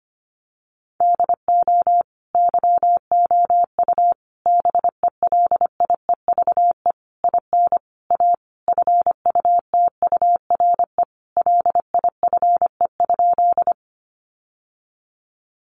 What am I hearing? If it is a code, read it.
Morse code, heard as a keyed tone, about 25 wpm, 700 Hz. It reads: DO YOU BELIEVE IN A FUTURE LIFE?
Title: Sentences from Top 1000 Words 25wpm